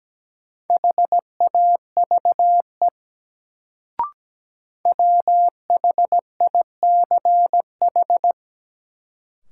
Morse code, heard as a keyed tone, about 17 wpm, 700 Hz. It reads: HAVE WHICH
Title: Sets of 2 Words - New Word Drill - 17wpm